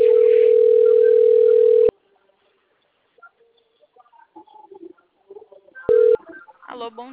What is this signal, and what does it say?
Ringback tone (North America), 440 + 480 Hz, with speech.
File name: out- - - - - .322